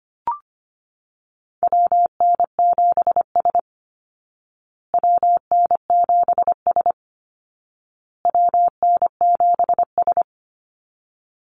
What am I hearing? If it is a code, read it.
Morse code, heard as a keyed tone, about 25 wpm, 700 Hz. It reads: WN7S WN7S WN7S
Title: Call Signs Repeated 2x - 25wpm